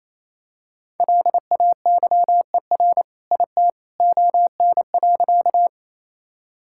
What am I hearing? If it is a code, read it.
Morse code, heard as a keyed tone, about 28 wpm, 700 Hz. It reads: LAYER IT ON.
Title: Sentences from Top 1600 Words 28wpm